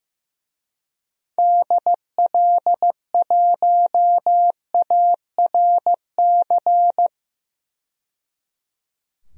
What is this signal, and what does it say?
Morse code, heard as a keyed tone, about 15 wpm, 700 Hz. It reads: DL1ARC
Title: DXCC-100 Call Signs - 6 Characters - 15wpm